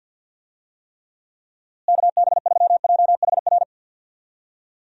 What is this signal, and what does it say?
Morse code, heard as a keyed tone, about 51 wpm, 700 Hz. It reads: KB3JHR
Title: USA Call Signs - 6 Characters - Repeated 2x - 50wpm